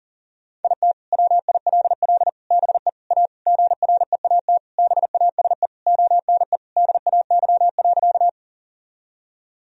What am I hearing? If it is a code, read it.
Morse code, heard as a keyed tone, about 37 wpm, 700 Hz. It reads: IT WILL BE A GREAT BASE ONE DAY.
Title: Sentences from Top 1000 Words - Review Part 1 - 40wpm